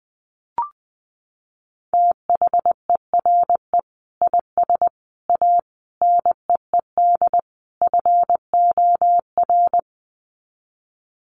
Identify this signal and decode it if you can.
Morse code, heard as a keyed tone, about 19 wpm, 700 Hz. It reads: THERE IS A NEED FOR
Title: Sets of 5 Words Encore2 20wpm